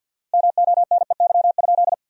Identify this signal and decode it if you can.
Morse code, heard as a keyed tone, about 47 wpm, 700 Hz. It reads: MONEY?
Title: 4x Word-Spacing - Sentences from Top 400 Words - 50wpm